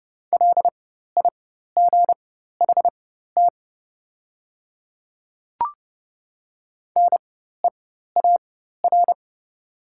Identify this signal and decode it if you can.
Morse code, heard as a keyed tone, about 30 wpm, spaced as for 14 wpm, 700 Hz. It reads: LIGHT NEAR
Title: 4x Character Spacing - Top 300 Words - 30wpm